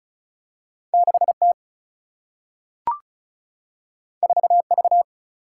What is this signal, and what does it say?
Morse code, heard as a keyed tone, about 36 wpm, 700 Hz. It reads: 6T 4V
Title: Single-Letter Single-Number 35wpm